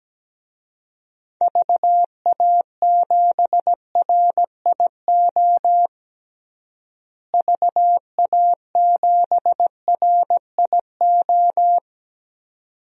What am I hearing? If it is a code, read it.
Morse code, heard as a keyed tone, about 17 wpm, 700 Hz. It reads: VA7RIO VA7RIO
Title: DXCC-100 Call Signs - 6 Characters - Repeated 2x - 17wpm